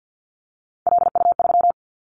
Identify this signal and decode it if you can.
Morse code, heard as a keyed tone, about 50 wpm, 700 Hz. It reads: LU3
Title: Callsign Prefixes for Top 50 DXCC - 50wpm